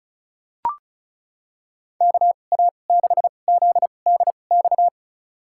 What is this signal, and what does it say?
Morse code, heard as a keyed tone, about 34 wpm, spaced as for 25 wpm, 700 Hz. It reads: KA6ZDX
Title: USA Call Signs - 2x Character Spacing - 35wpm